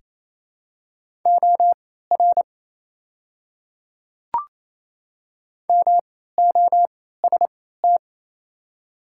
Morse code, heard as a keyed tone, about 27 wpm, spaced as for 16 wpm, 700 Hz. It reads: OR MOST
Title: 3x Character Spacing - Top 100 Words - 28wpm